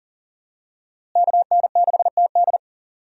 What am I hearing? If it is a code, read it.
Morse code, heard as a keyed tone, about 40 wpm, 700 Hz. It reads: KN6TD
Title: USA Call Signs - 5 Characters - Repeated 2x - 40wpm